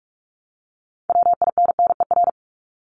Morse code, heard as a keyed tone, about 44 wpm, 700 Hz. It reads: WINNER
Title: Top 1751-2000 Words 45wpm